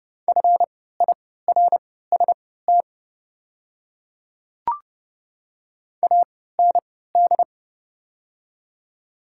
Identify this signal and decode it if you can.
Morse code, heard as a keyed tone, about 30 wpm, spaced as for 17 wpm, 700 Hz. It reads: FIRST AND
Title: 3x Character Spacing - Top 100 Words - 30wpm